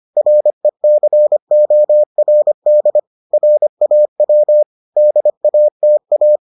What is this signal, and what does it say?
Morse code, heard as a keyed tone, about 25 wpm, 600 Hz. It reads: RECORD RAW DATA